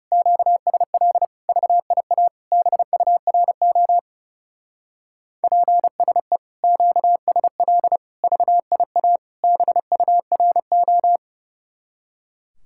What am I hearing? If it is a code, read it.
Morse code, heard as a keyed tone, about 34 wpm, 700 Hz. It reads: QSL VIA BURO PSE QSL VIA BURO
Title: Speed-Racing Common QSO Phrases Encore - 30wpm